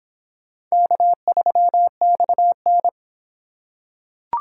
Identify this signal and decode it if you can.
Morse code, heard as a keyed tone, about 25 wpm, 700 Hz. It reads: K3XN